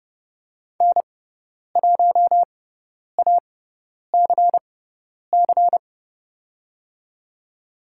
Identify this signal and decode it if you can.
Morse code, heard as a keyed tone, about 30 wpm, spaced as for 10 wpm, 700 Hz. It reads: N1ACC
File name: Lesson 17g - Letter u - Call Signs 30wpm